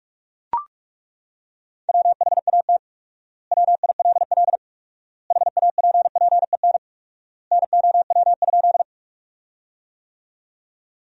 Morse code, heard as a keyed tone, about 41 wpm, 700 Hz. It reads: WHAT WILL HAPPEN NOW?
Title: 4x Word-Spacing - Sentences from Top 400 Words - 45wpm